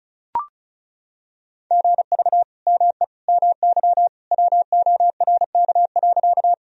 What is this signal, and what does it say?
Morse code, heard as a keyed tone, about 34 wpm, 700 Hz. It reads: GV ME MY WORK.
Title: Sentences from Top 200 Words and Abbreviations 35wpm